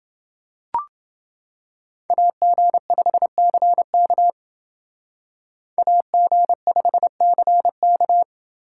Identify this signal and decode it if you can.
Morse code, heard as a keyed tone, about 28 wpm, 700 Hz. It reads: AG5CK AG5CK